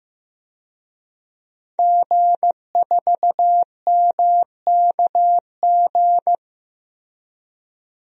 Morse code, heard as a keyed tone, about 15 wpm, 700 Hz. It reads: G4MKG
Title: DXCC-100 Call Signs - 5 Characters - Repeated 3x - 15wpm